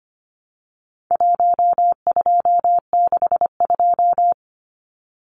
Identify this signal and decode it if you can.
Morse code, heard as a keyed tone, about 25 wpm, 700 Hz. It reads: 1262
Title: Numbers - 4-Digit 25wpm